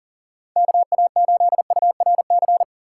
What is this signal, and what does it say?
Morse code, heard as a keyed tone, about 40 wpm, 700 Hz. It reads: KA8URC
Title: USA Call Signs - 6 Characters - 6x Character Spacing - 40wpm